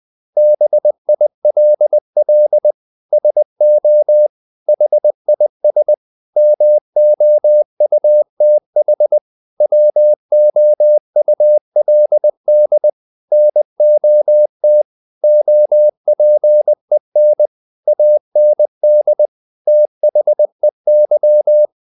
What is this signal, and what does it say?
Morse code, heard as a keyed tone, about 20 wpm, 600 Hz. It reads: BILL SO HIS MOUTH WOULD NOT OPEN AND THEY
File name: the old man and the sea part 2 20wpm